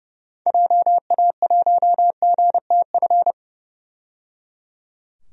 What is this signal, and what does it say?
Morse code, heard as a keyed tone, about 30 wpm, 700 Hz. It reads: JA1GTF
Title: DXCC-100 Call Signs - 6 Characters - 30wpm